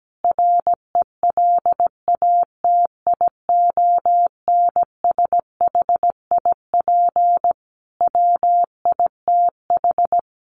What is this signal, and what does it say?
Morse code, heard as a keyed tone, about 17 wpm, 700 Hz. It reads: RELATIONSHIP WITH
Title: Sentences from Top 1100 Words 17wpm